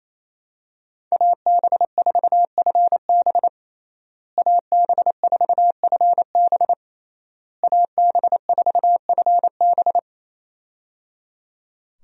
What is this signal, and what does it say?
Morse code, heard as a keyed tone, about 28 wpm, 700 Hz. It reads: AB4FB AB4FB AB4FB
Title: USA Call Signs - 5 Characters - Repeated 3x - 28wpm